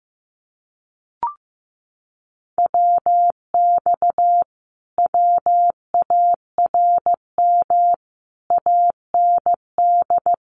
Morse code, heard as a keyed tone, about 15 wpm, 700 Hz. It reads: WX WARM AND
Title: Common QSO Phrases Encore 15wpm